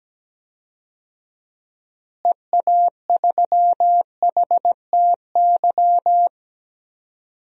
Morse code, heard as a keyed tone, about 17 wpm, 700 Hz. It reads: EA3HTY
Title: DXCC-100 Call Signs - 6 Characters - Repeated 2x - 17wpm